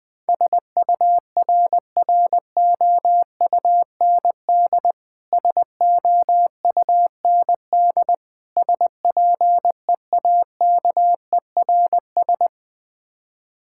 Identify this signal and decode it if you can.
Morse code, heard as a keyed tone, about 20 wpm, 700 Hz. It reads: SURROUND SOUND SPEAKERS
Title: Difficult Words - Speed-Racing with Hints - 20wpm